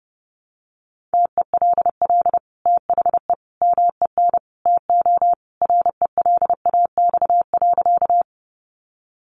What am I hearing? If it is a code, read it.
Morse code, heard as a keyed tone, about 30 wpm, 700 Hz. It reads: TELL THE MEN TO RELAX.
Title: Sentences from Top 1400 Words 30wpm